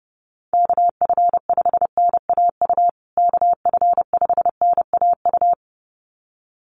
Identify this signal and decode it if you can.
Morse code, heard as a keyed tone, about 30 wpm, 700 Hz. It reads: KF5NAU KF5NAU
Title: USA Call Signs Repeated 2x 30wpm